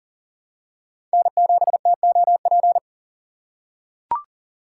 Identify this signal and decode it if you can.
Morse code, heard as a keyed tone, about 40 wpm, 700 Hz. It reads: N7TOP
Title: USA Call Signs - 3x Character Spacing - 40wpm